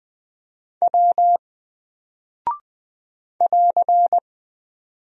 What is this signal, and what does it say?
Morse code, heard as a keyed tone, about 20 wpm, 700 Hz. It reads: W <AR>
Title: Single Character (Letter:Number:Punctuation:Prosign) 20wpm